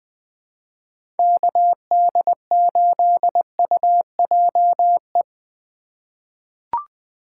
Morse code, heard as a keyed tone, about 20 wpm, 700 Hz. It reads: KD8UJE
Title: USA Call Signs - 3x Character Spacing - 20wpm